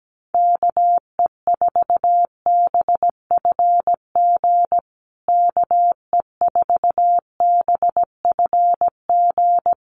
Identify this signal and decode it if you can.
Morse code, heard as a keyed tone, about 17 wpm, 700 Hz. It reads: KE4BFG KE4BFG
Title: USA Call Signs Repeated 3x 17wpm